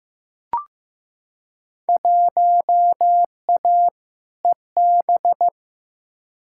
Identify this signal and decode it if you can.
Morse code, heard as a keyed tone, about 15 wpm, 700 Hz. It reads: 1A EB
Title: Field Day Exchanges - 15wpm